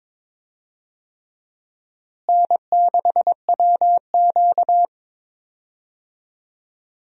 Morse code, heard as a keyed tone, about 22 wpm, 700 Hz. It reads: N6WQ
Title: USA Call Signs - 4 Characters - Repeated 3x - 22wpm